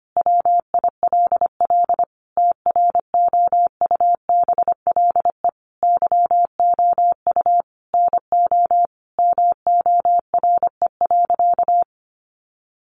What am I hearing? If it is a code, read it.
Morse code, heard as a keyed tone, about 25 wpm, 700 Hz. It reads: WILL TROUBLE YOU NO MORE.
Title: Sentences from Top 1200 Words 25wpm